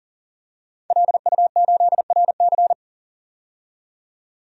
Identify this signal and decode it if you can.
Morse code, heard as a keyed tone, about 40 wpm, 700 Hz. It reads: LU8RC
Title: DXCC-100 Call Signs - 5 Characters - 40wpm